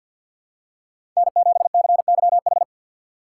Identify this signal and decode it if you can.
Morse code, heard as a keyed tone, about 50 wpm, 700 Hz. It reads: N7CYH
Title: USA Call Signs - 4x Character Spacing - 50wpm